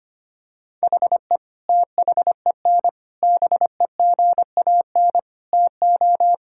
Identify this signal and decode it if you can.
Morse code, heard as a keyed tone, about 25 wpm, 700 Hz. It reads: HE THEN BEGAN TO
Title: Sentences from Top 1000 Words - Review Part 6 - 25wpm